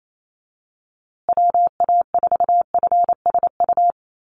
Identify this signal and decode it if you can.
Morse code, heard as a keyed tone, about 28 wpm, 700 Hz. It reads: WA4FSU
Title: USA Call Signs Repeated 2x 28wpm